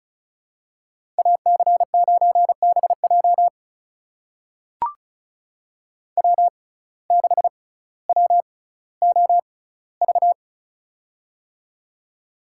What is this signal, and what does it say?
Morse code, heard as a keyed tone, about 36 wpm, spaced as for 18 wpm, 700 Hz. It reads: AC9BJ W6WOV
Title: USA Call Signs - 5 Characters - 6x Character Spacing - 35wpm